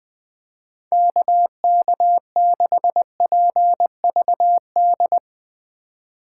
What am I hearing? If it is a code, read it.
Morse code, heard as a keyed tone, about 20 wpm, 700 Hz. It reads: KK6PVD
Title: USA Call Signs - 6 Characters - 4x Character Spacing - 20wpm